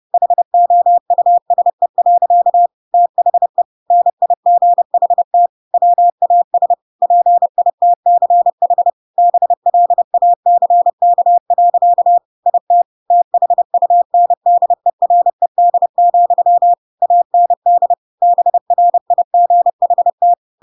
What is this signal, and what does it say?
Morse code, heard as a keyed tone, about 30 wpm, 700 Hz. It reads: HOUSE. THE NIGHT WAS PITCH BLACK. IT THUNDERED, AND BRIGHT
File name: Pinocchio / Pinocchio ch6 30 30 ik1hge